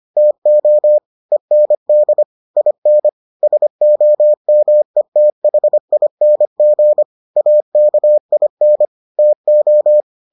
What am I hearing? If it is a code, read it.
Morse code, heard as a keyed tone, about 25 wpm, 600 Hz. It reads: TO END IN SOMETHING AKIN TO